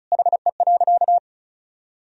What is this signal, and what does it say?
Morse code, heard as a keyed tone, about 36 wpm, 700 Hz. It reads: HE.
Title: Sentences from Top 400 Words and Abbreviations 35wpm